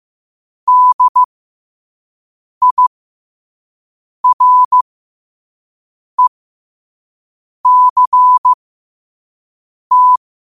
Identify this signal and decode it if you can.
Morse code, heard as a keyed tone, about 15 wpm, spaced as for 5 wpm, 1000 Hz. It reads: DIRECT
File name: qotdfiles / 5wpm